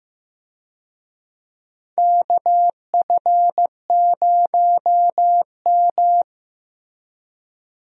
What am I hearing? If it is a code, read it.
Morse code, heard as a keyed tone, about 15 wpm, 700 Hz. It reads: KF0M
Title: USA Call Signs - 4 Characters - Repeated 2x - 15wpm